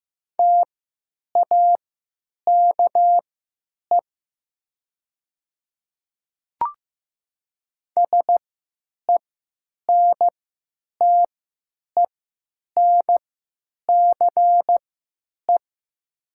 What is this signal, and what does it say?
Morse code, heard as a keyed tone, about 15 wpm, spaced as for 9 wpm, 700 Hz. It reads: TAKE SENTENCE
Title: 3x Character Spacing - Top 200 Words - 15wpm